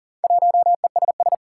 Morse code, heard as a keyed tone, about 41 wpm, 700 Hz. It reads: 1ESS
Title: Four Challenging Letter-Number Combinations 40wpm